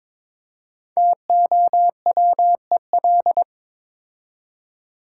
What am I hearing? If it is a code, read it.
Morse code, heard as a keyed tone, about 22 wpm, 700 Hz. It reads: TOWEL
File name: 5 Letter Words - First 3 Letters Given - 22wpm